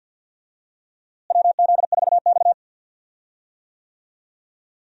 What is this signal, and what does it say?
Morse code, heard as a keyed tone, about 51 wpm, 700 Hz. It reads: WZ4X
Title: USA Call Signs - 4 Characters - Repeated 3x - 50wpm